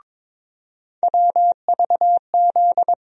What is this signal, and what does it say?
Morse code, heard as a keyed tone, about 22 wpm, 700 Hz. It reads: WVZ
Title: USA Call Sign Postfixes - 3 Letters 22wpm